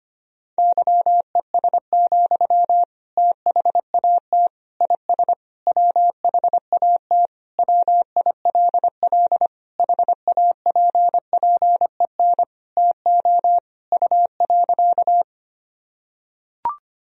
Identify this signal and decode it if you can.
Morse code, heard as a keyed tone, about 25 wpm, 700 Hz. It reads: YES, THAT IS WHAT WILL HAPPEN TO U.
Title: Sentences from Top 500 Words and Abbreviations 25wpm